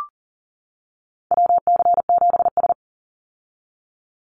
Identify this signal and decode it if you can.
Morse code, heard as a keyed tone, about 37 wpm, 700 Hz. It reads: WC7S
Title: Call Signs - 40wpm